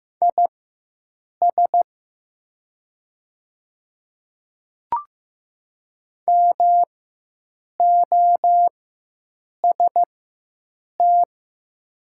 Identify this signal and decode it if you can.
Morse code, heard as a keyed tone, about 15 wpm, spaced as for 7 wpm, 700 Hz. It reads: IS MOST
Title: 4x Character Spacing - Top 100 Words - 15wpm